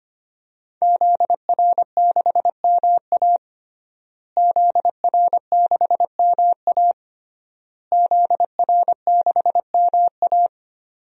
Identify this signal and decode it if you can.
Morse code, heard as a keyed tone, about 25 wpm, 700 Hz. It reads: ZR6MA ZR6MA ZR6MA
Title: DXCC-100 Call Signs - 5 Characters - Repeated 3x - 25wpm